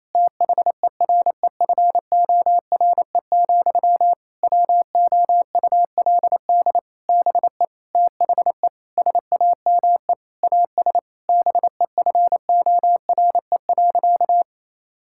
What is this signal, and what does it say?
Morse code, heard as a keyed tone, about 28 wpm, 700 Hz. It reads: THEREFORE, WOULD BE THE SAME AS BEFORE.
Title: Sentences from Top 1000 Words - Review Part 5 - 28wpm